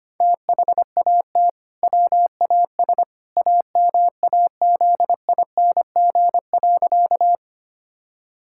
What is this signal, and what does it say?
Morse code, heard as a keyed tone, about 25 wpm, 700 Hz. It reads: THAT WAS AMAZING.
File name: practice / 25wpm / Sentences with 2-Word Phrases - 25wpm